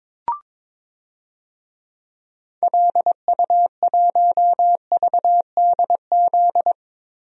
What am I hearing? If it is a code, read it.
Morse code, heard as a keyed tone, about 21 wpm, 700 Hz. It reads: LU1VDZ
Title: DXCC-100 Call Signs - 6 Characters - 22wpm